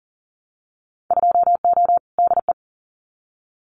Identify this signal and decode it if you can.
Morse code, heard as a keyed tone, about 40 wpm, 700 Hz. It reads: 2O DE
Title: Winter Field Day Exchanges Encore 40wpm